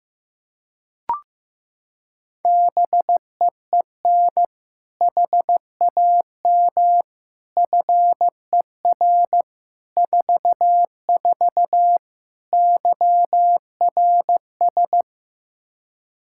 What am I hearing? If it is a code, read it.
Morse code, heard as a keyed tone, about 15 wpm, 700 Hz. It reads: BEEN HAM FER 44 YRS
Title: Practice QSO - Exchange 3 - More Info - 15wpm